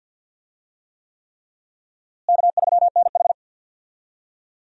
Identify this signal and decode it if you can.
Morse code, heard as a keyed tone, about 51 wpm, 700 Hz. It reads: K3NH
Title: USA Call Signs - 4 Characters - Repeated 3x - 50wpm